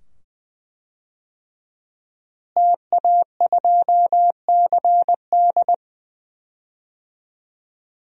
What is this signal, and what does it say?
Morse code, heard as a keyed tone, about 21 wpm, 700 Hz. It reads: TA2CD
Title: DXCC-100 Call Signs - 5 Characters - Repeated 2x - 20wpm